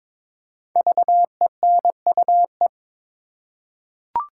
Morse code, heard as a keyed tone, about 22 wpm, 700 Hz. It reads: VENUE
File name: 5 Letter Words - First 3 Letters Given - 22wpm